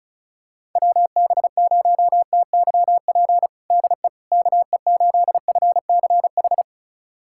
Morse code, heard as a keyed tone, about 36 wpm, 690 Hz. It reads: WB0TYP DE KE8FCH
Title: USA Callsign DE Callsign - 6 Characters - 35wpm